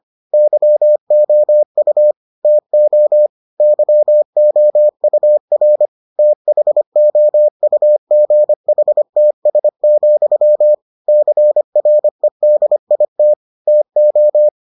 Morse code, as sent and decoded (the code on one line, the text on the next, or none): -.-- --- ..- / - --- / -.-- --- ..- .-. / - .... --- ..- --. .... - ... --..-- / -.-. .-. -.. .. - / - ---
YOU TO YOUR THOUGHTS, CREDIT TO